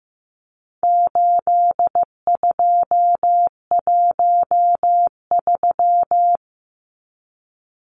Morse code, heard as a keyed tone, about 15 wpm, 700 Hz. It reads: 8213
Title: Numbers - 4-Digit 15wpm